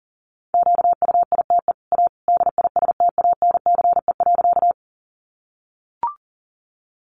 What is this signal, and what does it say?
Morse code, heard as a keyed tone, about 37 wpm, 700 Hz. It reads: QUITE A DISTANCE.
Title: Sentences from Top 1200 Words 40wpm